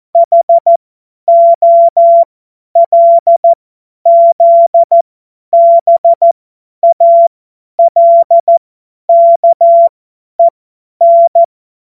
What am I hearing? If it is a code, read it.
Morse code, heard as a keyed tone, about 14 wpm, spaced as for 10 wpm, 680 Hz. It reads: HOLZBALKEN
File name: Morsekurs / Witze / Witz 17.Oktober 2023 50 F70